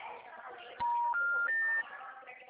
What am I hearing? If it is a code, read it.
Special information tone, with speech.
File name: out- -2023- - - .76573